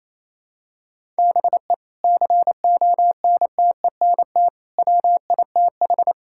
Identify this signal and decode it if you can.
Morse code, heard as a keyed tone, about 28 wpm, 700 Hz. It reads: BE CONTENT WITH
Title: Sentences from Top 1000 Words - Review Part 6 - 28wpm